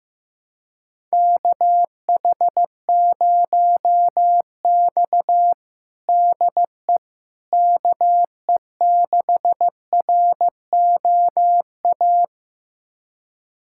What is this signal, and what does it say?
Morse code, heard as a keyed tone, about 15 wpm, 700 Hz. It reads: KH0X DE KE6ROA
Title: USA Callsign DE Callsign 15wpm